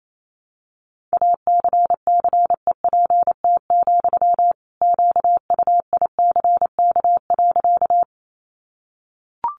Morse code, heard as a keyed tone, about 28 wpm, 700 Hz. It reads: ACCEPT, QUICK.
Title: Sentences from Top 1300 Words 28wpm